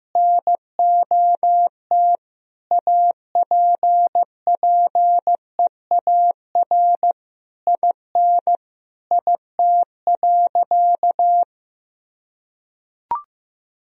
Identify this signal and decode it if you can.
Morse code, heard as a keyed tone, about 15 wpm, 700 Hz. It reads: NOT APPEAR IN IT.
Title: Sentences from Top 1000 Words - Review Part 2 - 15wpm